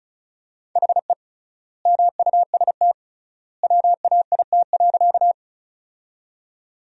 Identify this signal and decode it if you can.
Morse code, heard as a keyed tone, about 34 wpm, 700 Hz. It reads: HE MUST WAIT.
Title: 3x Word-Spacing - Sentences from Top 500 Words - 35wpm